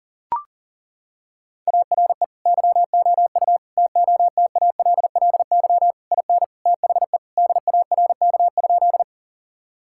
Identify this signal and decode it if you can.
Morse code, heard as a keyed tone, about 40 wpm, 700 Hz. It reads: ARE YOU TOTALLY IN THE DARK?
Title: Sentences from Top 1600 Words 40wpm